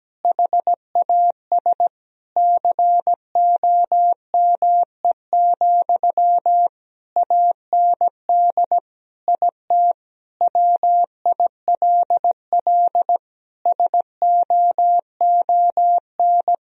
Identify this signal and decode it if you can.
Morse code, heard as a keyed tone, about 17 wpm, 700 Hz. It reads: HAS COME, AND IT WILL SOON
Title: Sentences from Top 1000 Words - Review Part 3 - 17wpm